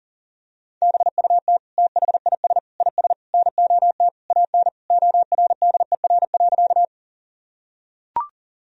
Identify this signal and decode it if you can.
Morse code, heard as a keyed tone, about 40 wpm, 700 Hz. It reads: BUT THIS IS NOT AN ORDER.
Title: Sentences from Top 1000 Words - Review Part 2 - 40wpm